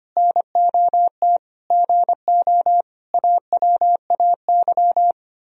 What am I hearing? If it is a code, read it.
Morse code, heard as a keyed tone, about 32 wpm, 700 Hz. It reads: NOT GO AWAY